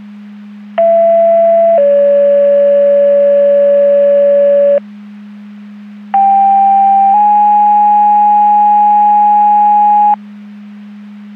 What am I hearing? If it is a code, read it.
Two-tone page, heard as a one-second tone, then a three-second tone, about 670 then 570 Hz, about 790 then 830 Hz. It is heard from 0.8 to 10.1 s.